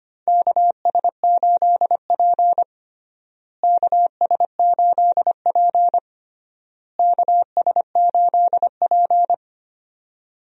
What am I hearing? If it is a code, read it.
Morse code, heard as a keyed tone, about 25 wpm, 700 Hz. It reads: KS8P KS8P KS8P
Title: USA Call Signs - 4 Characters - Repeated 3x - 25wpm